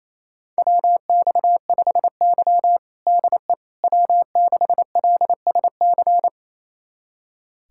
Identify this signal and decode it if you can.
Morse code, heard as a keyed tone, about 27 wpm, 700 Hz. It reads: WX5Y DE W6LSC
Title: USA Callsign DE Callsign 28wpm